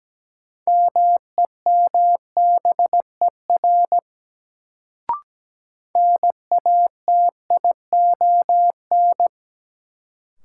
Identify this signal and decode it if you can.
Morse code, heard as a keyed tone, about 17 wpm, 700 Hz. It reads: MEMBER NATION
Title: Sets of 3 Words - New Word Drill - 17wpm